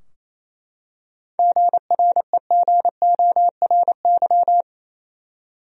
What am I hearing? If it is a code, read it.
Morse code, heard as a keyed tone, about 28 wpm, 700 Hz. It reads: GREGORY
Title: USA Common Names - 28wpm